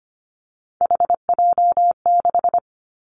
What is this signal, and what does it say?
Morse code, heard as a keyed tone, about 25 wpm, 700 Hz. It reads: HJ6
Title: Callsign Prefixes for Top 100 DXCC - 25wpm